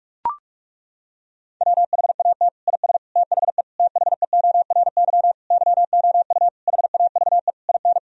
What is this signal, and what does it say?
Morse code, heard as a keyed tone, about 41 wpm, 700 Hz. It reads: WHAT IS THE THEORY YOU HAVE IN
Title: Sentences from Top 1000 Words - Review Part 3 - 45wpm